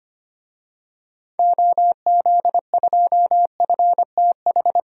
Morse code, heard as a keyed tone, about 25 wpm, 700 Hz. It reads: OZ2FTH
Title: DXCC-100 Call Signs - 6 Characters - 25wpm